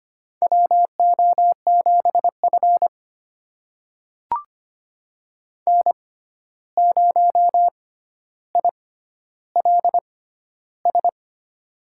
Morse code, heard as a keyed tone, about 25 wpm, spaced as for 12 wpm, 700 Hz. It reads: WO7F N0ILS
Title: USA Call Signs - 6x Character Spacing - 25wpm